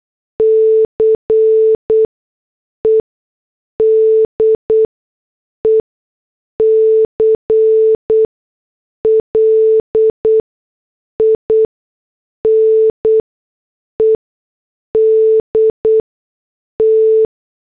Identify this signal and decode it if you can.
Morse code, heard as a keyed tone, about 8 wpm, spaced as for 6 wpm, 440 Hz. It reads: CE DECLINED T